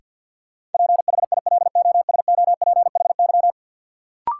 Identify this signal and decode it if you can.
Morse code, heard as a keyed tone, about 48 wpm, 700 Hz. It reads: PHILOSOPHY
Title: Sentences from Top 900 Words - New Word Drill - 50wpm